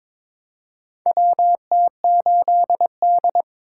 Morse code, heard as a keyed tone, about 22 wpm, 700 Hz. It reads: WT8D
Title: USA Call Signs - 4 Characters - 2x Character Spacing - 22wpm